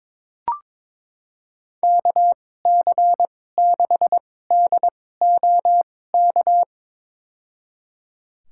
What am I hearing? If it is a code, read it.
Morse code, heard as a keyed tone, about 22 wpm, spaced as for 16 wpm, 700 Hz. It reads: KC6DOK
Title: USA Call Signs - 6 Characters - 2x Character Spacing - 22wpm